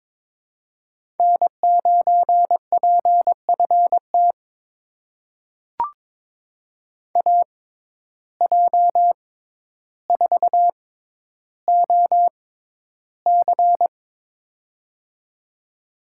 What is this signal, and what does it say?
Morse code, heard as a keyed tone, about 21 wpm, spaced as for 11 wpm, 700 Hz. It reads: N9PFT AJ4OC
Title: USA Call Signs - 5 Characters - 6x Character Spacing - 22wpm